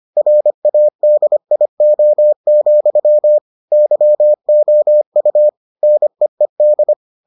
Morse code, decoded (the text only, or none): RADIO, YOU NEED